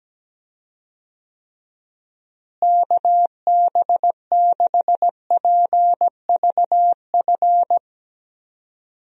Morse code, heard as a keyed tone, about 17 wpm, 700 Hz. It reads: KB6PVF
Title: USA Call Signs - 6 Characters - Repeated 2x - 17wpm